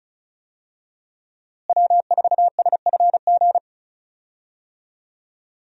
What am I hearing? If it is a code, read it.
Morse code, heard as a keyed tone, about 34 wpm, 700 Hz. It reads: W4SFG
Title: USA Call Signs - 5 Characters - Repeated 2x - 35wpm